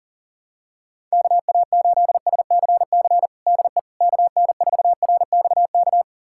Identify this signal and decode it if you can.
Morse code, heard as a keyed tone, about 40 wpm, 700 Hz. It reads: KA8SCC DE KN4RXK
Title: USA Callsign DE Callsign 40wpm